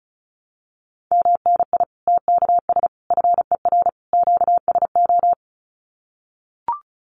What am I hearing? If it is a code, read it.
Morse code, heard as a keyed tone, about 34 wpm, 700 Hz. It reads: MNI TKS FER QSO
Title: Common QSO Phrases 35wpm